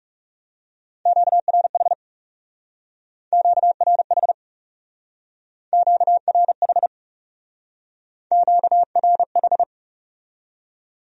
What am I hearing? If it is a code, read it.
Morse code, heard as a keyed tone, about 37 wpm, 700 Hz. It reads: QRH QRH QRH QRH
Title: Speed-Racing Q-Codes - 30wpm